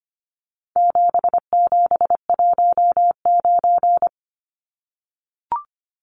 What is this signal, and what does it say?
Morse code, heard as a keyed tone, about 25 wpm, 700 Hz. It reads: 7719
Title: Numbers - 4-Digit 25wpm